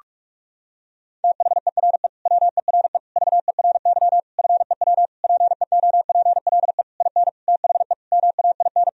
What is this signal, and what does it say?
Morse code, heard as a keyed tone, about 46 wpm, 700 Hz. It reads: THERE WERE VERY FEW PEOPLE IN THE MAIN
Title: Sentences from Top 1000 Words - Review Part 2 - 45wpm